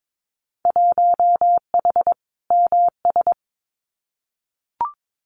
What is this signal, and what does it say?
Morse code, heard as a keyed tone, about 22 wpm, 700 Hz. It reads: 1H MS
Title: Winter Field Day Exchanges 22wpm